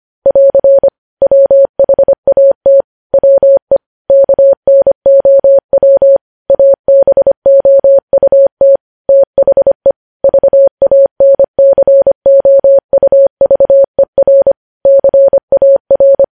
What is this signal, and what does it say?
Morse code, heard as a keyed tone, about 25 wpm, 550 Hz. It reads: <AR> WHAT WE KNOW ABOUT THE VANCOUVER CAR
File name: podcast / News headlines-1x-25x25